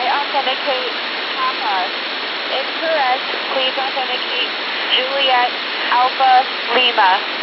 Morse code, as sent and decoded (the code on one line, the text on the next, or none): none